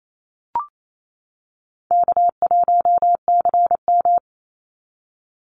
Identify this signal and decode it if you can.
Morse code, heard as a keyed tone, about 28 wpm, 700 Hz. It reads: K1CM
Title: NA SOTA Chaser Callsigns 28wpm